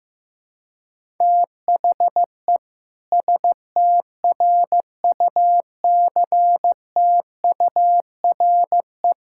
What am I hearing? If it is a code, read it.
Morse code, heard as a keyed tone, about 15 wpm, 700 Hz. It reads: THE STRUCTURE
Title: Sentences from Top 1000 Words - Review Part 6 - 15wpm